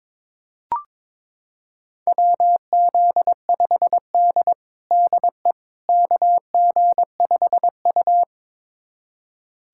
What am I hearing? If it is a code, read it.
Morse code, heard as a keyed tone, about 22 wpm, 700 Hz. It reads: WZ5D DE KG5U
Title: USA Callsign DE Callsign - 4 Characters - 22wpm